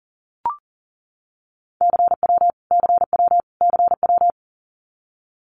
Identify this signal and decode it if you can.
Morse code, heard as a keyed tone, about 38 wpm, 700 Hz. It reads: CW CW CW
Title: Abbreviations Repeated 3x - 40wpm